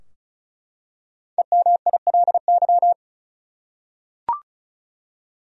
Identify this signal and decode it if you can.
Morse code, heard as a keyed tone, about 34 wpm, 700 Hz. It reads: EMILY
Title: USA Common Names - 35wpm